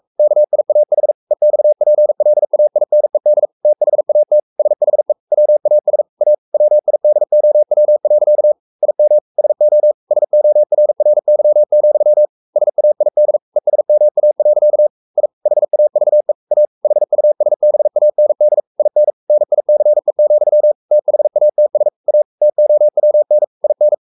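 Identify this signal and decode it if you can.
Morse code, heard as a keyed tone, about 44 wpm, 600 Hz. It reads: KIAH EXPLAINED THAT SHE WAS A WIDOW. IM SO SORRY, SAID ESMA. I HAVE A HUSBAND IN NICE, THATS A TOWN IN
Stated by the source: Morse code at 43 wpm sending Ken Follett's novel Never in English, from CW med Gnister.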